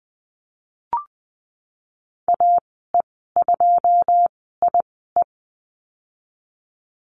Morse code, heard as a keyed tone, about 20 wpm, spaced as for 14 wpm, 700 Hz. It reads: AE2IE
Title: 082 Lesson 14g - Number 2 - Call Signs 20wpm